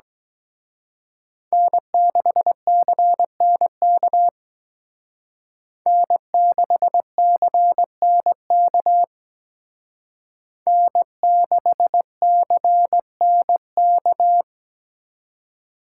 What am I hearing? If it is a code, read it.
Morse code, heard as a keyed tone, about 21 wpm, 700 Hz. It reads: N6CNK N6CNK N6CNK
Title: Speed Racing - USA 1x3 Call Signs 17wpm